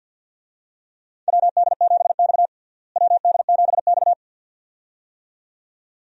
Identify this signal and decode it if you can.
Morse code, heard as a keyed tone, about 50 wpm, 700 Hz. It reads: WD7X WD7X
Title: USA Call Signs - 4 Characters - Repeated 2x - 50wpm